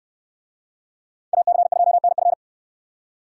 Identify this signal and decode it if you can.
Morse code, heard as a keyed tone, about 64 wpm, 700 Hz. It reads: AC2NF